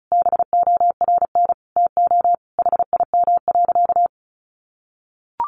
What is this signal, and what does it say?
Morse code, heard as a keyed tone, about 34 wpm, 700 Hz. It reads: BORN TO HIM.
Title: Sentences from Top 1500 Words 35wpm